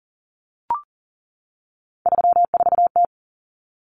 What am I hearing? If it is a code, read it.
Morse code, heard as a keyed tone, about 40 wpm, 700 Hz. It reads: 34T
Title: Common Cut Numbers - 3-Digit 40wpm